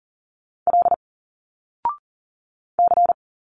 Morse code, heard as a keyed tone, about 40 wpm, 700 Hz. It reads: L C
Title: Single Letters 40wpm